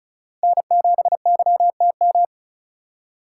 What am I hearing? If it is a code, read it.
Morse code, heard as a keyed tone, about 43 wpm, 700 Hz. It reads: N7YTM
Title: Speed Racing - USA 1x3 Call Signs 35wpm